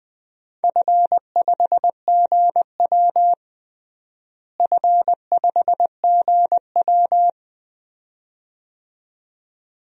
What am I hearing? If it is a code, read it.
Morse code, heard as a keyed tone, about 20 wpm, 700 Hz. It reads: F5GW F5GW
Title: DXCC-100 Call Signs - 4 Characters - Repeated 2x - 20wpm